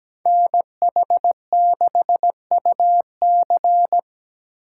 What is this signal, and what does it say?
Morse code, heard as a keyed tone, about 17 wpm, 700 Hz. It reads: NH6UC
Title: USA Call Signs - 4x Character Spacing - 17wpm